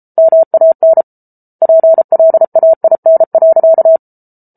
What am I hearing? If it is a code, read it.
Morse code, heard as a keyed tone, about 34 wpm, 650 Hz. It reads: MAN PLAIN.